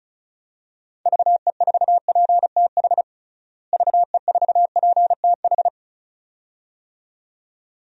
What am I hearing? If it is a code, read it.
Morse code, heard as a keyed tone, about 36 wpm, 700 Hz. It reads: VE4PTH VE4PTH
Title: DXCC-100 Call Signs - 6 Characters - Repeated 2x - 35wpm